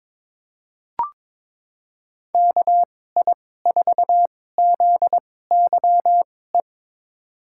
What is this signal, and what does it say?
Morse code, heard as a keyed tone, about 22 wpm, spaced as for 16 wpm, 700 Hz. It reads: KI4ZYE
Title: USA Call Signs - 6 Characters - 2x Character Spacing - 22wpm